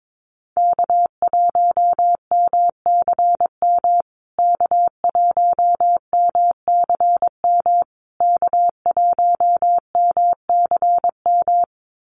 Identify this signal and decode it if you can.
Morse code, heard as a keyed tone, about 22 wpm, 700 Hz. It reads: K1MCM K1MCM K1MCM
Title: USA Call Signs Repeated 3x 22wpm